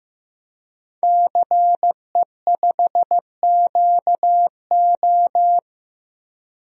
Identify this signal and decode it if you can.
Morse code, heard as a keyed tone, about 15 wpm, 700 Hz. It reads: CE5QO
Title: DXCC-100 Call Signs - 5 Characters - Repeated 2x - 15wpm